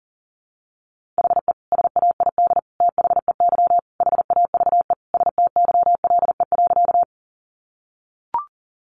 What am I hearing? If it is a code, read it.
Morse code, heard as a keyed tone, about 40 wpm, 700 Hz. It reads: HE SAID THEY HAVE STYLE.
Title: Sentences from Top 800 Words 40wpm